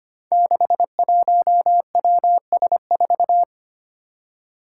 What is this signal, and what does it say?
Morse code, heard as a keyed tone, about 25 wpm, 700 Hz. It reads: 61WS4
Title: Five Challenging Letter-Number Combinations 25wpm